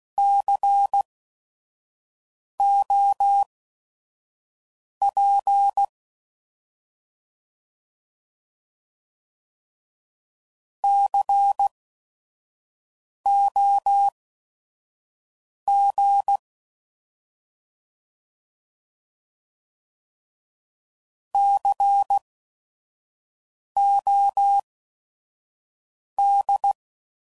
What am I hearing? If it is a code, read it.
Morse code, heard as a keyed tone, about 16 wpm, spaced as for 5 wpm, 800 Hz. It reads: COP COG COD
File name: Prefixes / By First Letter / Cs